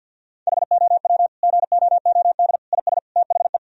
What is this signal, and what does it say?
Morse code, heard as a keyed tone, about 50 wpm, 700 Hz. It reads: HOW GOOD IS THE
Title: Sentences from Top 1000 Words - Review Part 4 - 50wpm